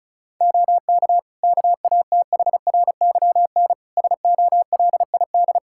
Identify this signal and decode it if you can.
Morse code, heard as a keyed tone, about 34 wpm, 700 Hz. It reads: OK KATHRYN SOLID
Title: K1USN SST QSO Practice 35wpm